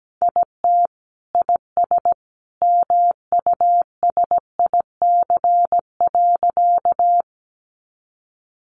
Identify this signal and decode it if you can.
Morse code, heard as a keyed tone, about 17 wpm, 700 Hz. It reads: IT IS MUSIC.
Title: Sentences from Top 300 Words 17wpm